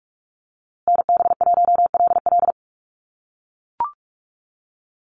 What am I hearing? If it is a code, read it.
Morse code, heard as a keyed tone, about 44 wpm, 700 Hz. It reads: NB1LL
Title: USA Call Signs Repeated 3x 45wpm